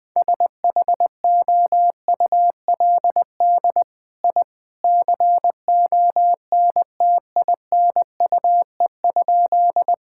Morse code, heard as a keyed tone, about 20 wpm, 700 Hz. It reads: SHOULD I CONTINUE?
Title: Sentences from Top 1000 Words - Review Part 6 - 20wpm